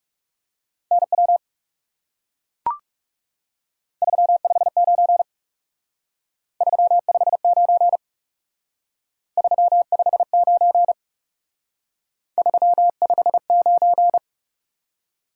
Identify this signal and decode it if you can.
Morse code, heard as a keyed tone, about 37 wpm, 700 Hz. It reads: NW 359 359 359 359
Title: Speed-Racing Top 180 QSO Elements 30wpm